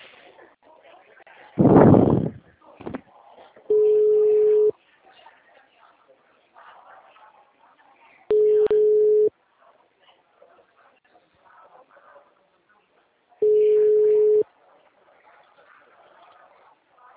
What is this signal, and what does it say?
Ringback tone (Europe, 425 Hz).